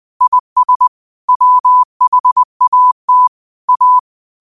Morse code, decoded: IS WHAT A

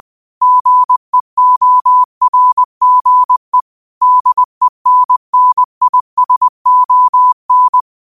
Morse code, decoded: GEORGE DENNISON

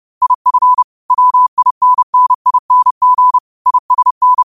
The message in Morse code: IF WINNING ISN

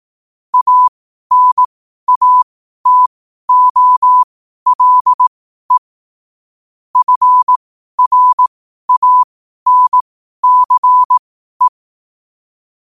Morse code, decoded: ANATOLE FRANCE

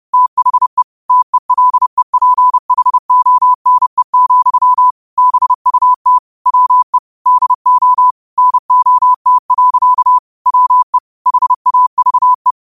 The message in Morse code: THE TELEPHONE, BUT WE DO NOT. WE HAVE

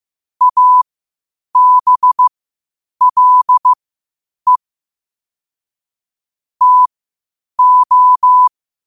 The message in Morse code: ABLE TO

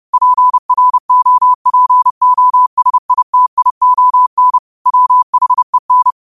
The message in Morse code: PROPOSITION WHEN

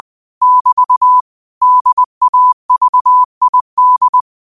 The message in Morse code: <BT> DAVID

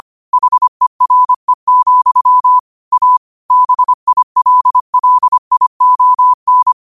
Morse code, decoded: HERE, A BILLION

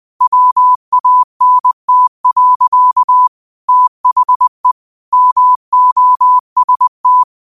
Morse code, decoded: WANT. THE MOST

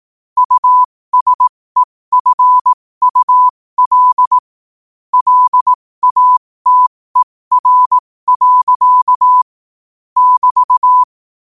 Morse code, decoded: USEFUL LATER. <BT>

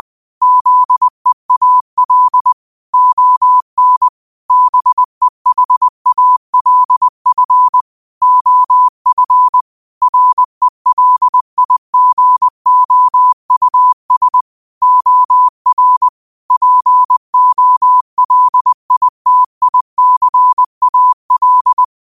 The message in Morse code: ZEAL ON BEHALF OF RELIGOUS OR POLITICAL

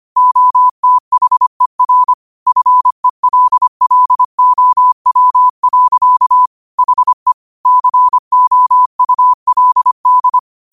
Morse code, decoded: OTHER FELLOW. HE COULD

